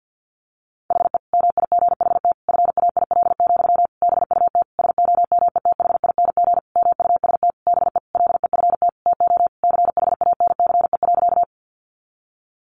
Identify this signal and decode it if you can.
Morse code, heard as a keyed tone, about 50 wpm, 700 Hz. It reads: HE MIGHT FAIL, BUT SOMETHING MUST BE LEFT TO CHANCE.
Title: Sentences from Top 1100 Words 50wpm